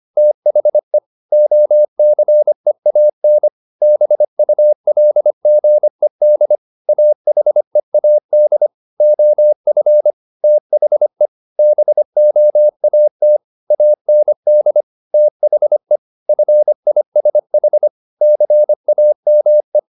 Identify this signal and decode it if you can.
Morse code, heard as a keyed tone, about 25 wpm, 600 Hz. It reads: THE OCEAN BULGED AHEAD OF THE BOAT AND THE FISH CAME
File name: the old man and the sea part 1 25wpm